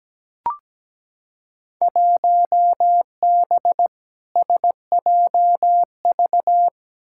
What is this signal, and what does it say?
Morse code, heard as a keyed tone, about 17 wpm, 700 Hz. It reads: 1B SJV
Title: Field Day Exchanges Encore - 17wpm